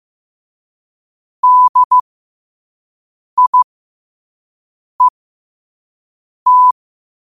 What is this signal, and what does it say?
Morse code, heard as a keyed tone, about 15 wpm, spaced as for 5 wpm, 1000 Hz. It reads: DIET
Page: New quotes every day in morse code at 5 Words per minute.